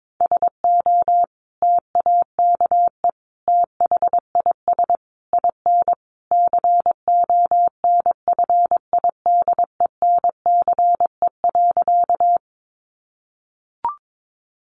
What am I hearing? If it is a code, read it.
Morse code, heard as a keyed tone, about 22 wpm, 700 Hz. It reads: SO TAKE THIS IN CONFIDENCE.
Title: Sentences from Top 1500 Words 22wpm